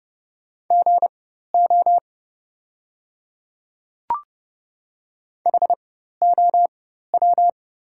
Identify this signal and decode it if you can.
Morse code, heard as a keyed tone, about 30 wpm, spaced as for 14 wpm, 700 Hz. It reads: GO HOW
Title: 4x Character Spacing - Top 200 Words - 30wpm